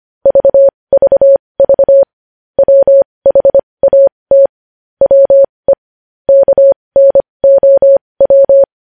Morse code, heard as a keyed tone, about 25 wpm, spaced as for 20 wpm, 550 Hz. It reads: VVV WHAT WE KNOW
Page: news headlines in morse code at 20 wpm